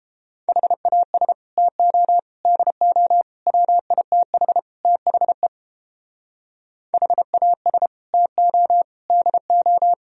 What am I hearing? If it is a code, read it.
Morse code, heard as a keyed tone, about 29 wpm, 700 Hz. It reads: HAS TO DO WITH THE HAS TO DO